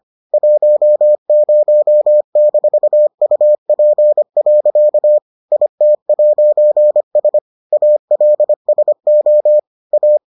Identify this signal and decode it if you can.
Morse code, heard as a keyed tone, about 25 wpm, 600 Hz. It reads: 10-UP. IT'S ALSO A